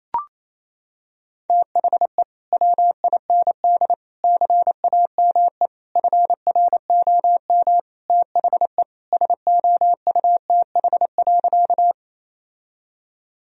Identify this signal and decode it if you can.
Morse code, heard as a keyed tone, about 28 wpm, 700 Hz. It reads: THE WIND CAME FROM THE SOUTH.
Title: Sentences from Top 1000 Words - Review Part 2 - 28wpm